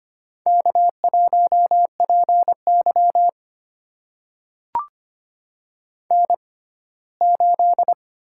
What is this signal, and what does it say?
Morse code, heard as a keyed tone, about 25 wpm, spaced as for 17 wpm, 700 Hz. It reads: K1PY N8
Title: USA Call Signs - 4 Characters - 6x Character Spacing - 25wpm